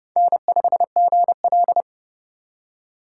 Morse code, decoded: N5GL

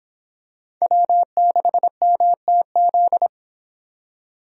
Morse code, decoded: W6MTZ

